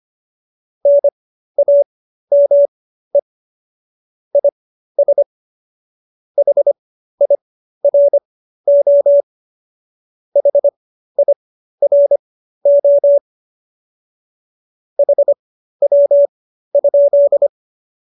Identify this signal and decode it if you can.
Morse code, heard as a keyed tone, about 25 wpm, spaced as for 13 wpm, 570 Hz. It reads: NAME IS HIRO HIRO HW?
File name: qso208b 13